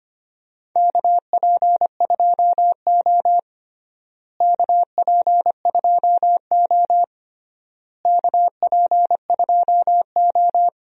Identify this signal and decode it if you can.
Morse code, heard as a keyed tone, about 25 wpm, 700 Hz. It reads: KP2O KP2O KP2O
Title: DXCC-100 Call Signs - 4 Characters - Repeated 3x - 25wpm